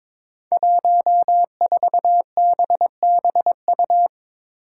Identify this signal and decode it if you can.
Morse code, heard as a keyed tone, about 22 wpm, 700 Hz. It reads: 14BBU
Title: Five Challenging Letter-Number Combinations 22wpm